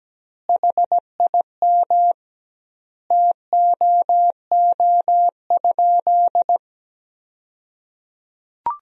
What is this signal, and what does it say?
Morse code, heard as a keyed tone, about 17 wpm, 700 Hz. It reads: HIM TOO?
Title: 2x Word-Spacing - Sentences from Top 500 Words - 17wpm